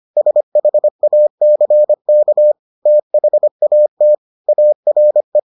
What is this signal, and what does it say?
Morse code, heard as a keyed tone, about 25 wpm, 600 Hz. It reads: SHACK THAT ARE